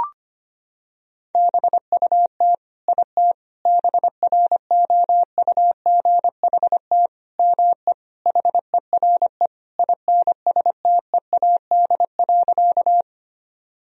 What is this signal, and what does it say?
Morse code, heard as a keyed tone, about 25 wpm, 700 Hz. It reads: BUT IT BROUGHT ME HERE INSTEAD.
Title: Sentences from Top 1000 Words - Review Part 4 - 25wpm